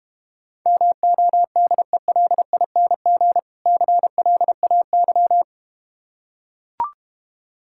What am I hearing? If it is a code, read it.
Morse code, heard as a keyed tone, about 32 wpm, 700 Hz. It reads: MODELING CLAY